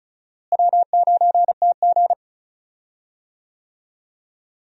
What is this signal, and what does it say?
Morse code, heard as a keyed tone, about 34 wpm, 700 Hz. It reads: W9TG
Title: USA Call Signs - 4 Characters - Repeated 2x - 35wpm